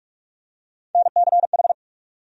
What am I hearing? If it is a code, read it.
Morse code, heard as a keyed tone, about 42 wpm, 700 Hz. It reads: NCH
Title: USA Call Sign Postfixes - 3 Letters 45wpm